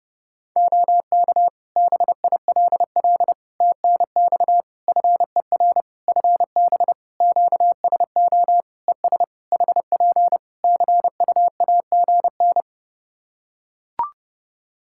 Morse code, heard as a keyed tone, about 30 wpm, 700 Hz. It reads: OK BILL TNX FER FB QSO ES HP CUAGN
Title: Practice QSO - Exchange 4 - Ending - 30wpm